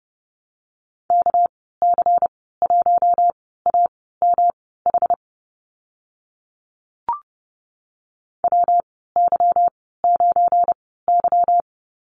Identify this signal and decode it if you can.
Morse code, heard as a keyed tone, about 30 wpm, spaced as for 17 wpm, 700 Hz. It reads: KC1AMH WY9Y
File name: Lesson 37g - Letter x - Call Signs 30wpm